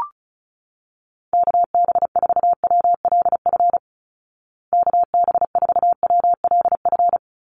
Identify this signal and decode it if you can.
Morse code, heard as a keyed tone, about 34 wpm, 700 Hz. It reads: KB4WLF KB4WLF
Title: Call Signs Repeated 3x - 35wpm